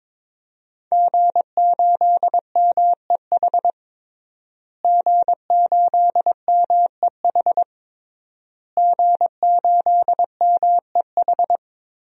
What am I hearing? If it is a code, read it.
Morse code, heard as a keyed tone, about 22 wpm, 700 Hz. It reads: G8MEH G8MEH G8MEH
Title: DXCC-100 Call Signs - 5 Characters - Repeated 3x - 22wpm